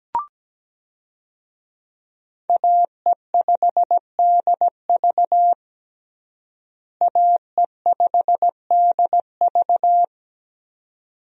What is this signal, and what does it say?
Morse code, heard as a keyed tone, about 17 wpm, 700 Hz. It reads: AE5DV AE5DV
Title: USA Call Signs - 5 Characters - Repeated 2x - 17wpm